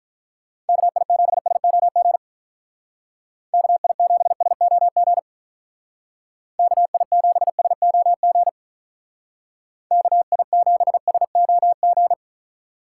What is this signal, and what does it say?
Morse code, heard as a keyed tone, about 44 wpm, 700 Hz. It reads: KI7SOG KI7SOG KI7SOG KI7SOG
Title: Speed Racing - USA 2x3 Call Signs 35wpm